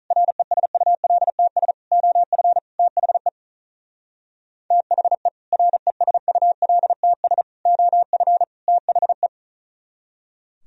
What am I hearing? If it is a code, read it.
Morse code, heard as a keyed tone, about 39 wpm, 700 Hz. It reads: RESULTS OF THE THE RESULTS OF THE
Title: Speed-Racing - Sets of 4 Words 35wpm